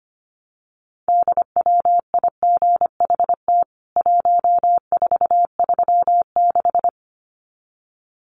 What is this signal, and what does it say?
Morse code, heard as a keyed tone, about 25 wpm, 700 Hz. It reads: DWIGHT 1436
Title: CWT Exchange 25wpm